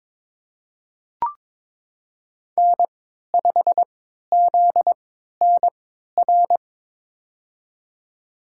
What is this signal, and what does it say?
Morse code, heard as a keyed tone, about 22 wpm, spaced as for 12 wpm, 700 Hz. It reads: N5ZNR
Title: USA Call Signs - 5 Characters - 3x Character Spacing - 22wpm